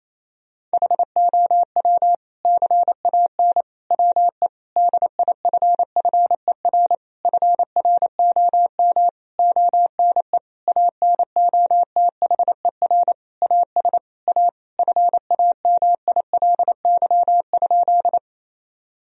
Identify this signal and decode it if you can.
Morse code, heard as a keyed tone, about 28 wpm, 700 Hz. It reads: HOW CAN WE DIFFER FROM ONE ANOTHER AS A FAMILY?
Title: Sentences from Top 1000 Words - Review Part 1 - 28wpm